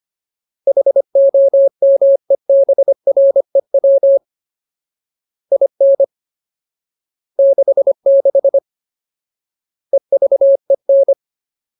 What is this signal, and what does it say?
Morse code, heard as a keyed tone, about 25 wpm, 550 Hz. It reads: HOMEBREW IN 66 EVEN
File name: Ultimate Word List / 25WPM Ultimate